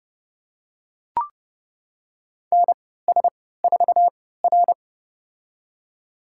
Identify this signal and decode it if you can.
Morse code, heard as a keyed tone, about 30 wpm, spaced as for 17 wpm, 700 Hz. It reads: NS4R
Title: USA Call Signs - 4 Characters - 3x Character Spacing - 30wpm